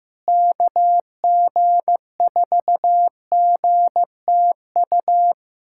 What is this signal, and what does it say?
Morse code, heard as a keyed tone, about 15 wpm, 700 Hz. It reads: KG4GTU
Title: USA Callsign DE Callsign - 6 Characters - 15wpm